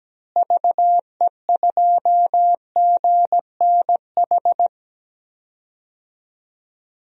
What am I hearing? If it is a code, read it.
Morse code, heard as a keyed tone, about 17 wpm, 700 Hz. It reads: VE2GNH
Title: DXCC-100 Call Signs - 6 Characters - Repeated 3x - 17wpm